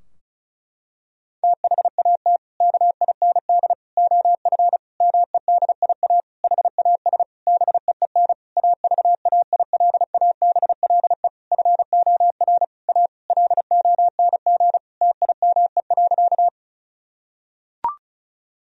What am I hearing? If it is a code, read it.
Morse code, heard as a keyed tone, about 36 wpm, 700 Hz. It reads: THAT KIND OF MEDIA HAS BEEN AVAILABLE FOR A LONG TIME.
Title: Sentences from Top 1000 Words - Review Part 4 - 35wpm